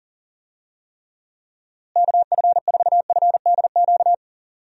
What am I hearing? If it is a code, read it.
Morse code, heard as a keyed tone, about 40 wpm, 700 Hz. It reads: KF4FDQ
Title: USA Call Signs - 6 Characters - Repeated 3x - 40wpm